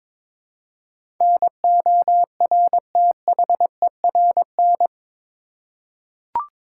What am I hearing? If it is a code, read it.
Morse code, heard as a keyed tone, about 22 wpm, 700 Hz. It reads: NORTHERN
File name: practice / 22wpm / Eight Letter Words - 22wpm